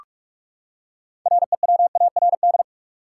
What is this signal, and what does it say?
Morse code, heard as a keyed tone, about 41 wpm, 700 Hz. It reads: REWARD
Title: Six Letter Words - 45wpm